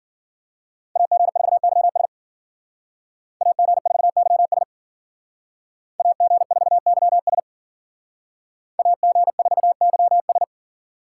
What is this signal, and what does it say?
Morse code, heard as a keyed tone, about 52 wpm, 700 Hz. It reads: AG4YS AG4YS AG4YS AG4YS